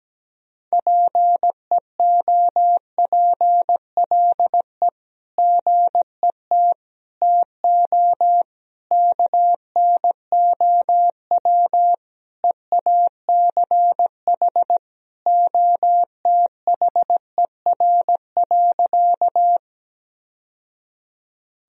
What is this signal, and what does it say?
Morse code, heard as a keyed tone, about 17 wpm, 700 Hz. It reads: PEOPLE GET TO KNOW EACH OTHER.
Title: Sentences from Top 1000 Words - Review Part 1 - 17wpm